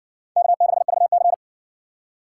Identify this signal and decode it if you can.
Morse code, heard as a keyed tone, about 56 wpm, 700 Hz. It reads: K6VC